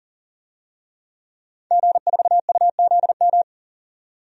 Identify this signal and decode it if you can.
Morse code, heard as a keyed tone, about 40 wpm, 700 Hz. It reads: G4UZM
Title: DXCC-100 Call Signs - 5 Characters - 40wpm